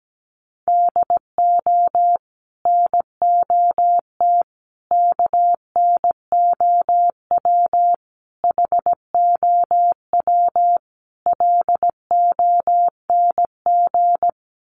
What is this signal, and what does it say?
Morse code, heard as a keyed tone, about 17 wpm, 700 Hz. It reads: DO NOT KNOW HOW LONG
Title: Sets of 5 Words Encore2 17wpm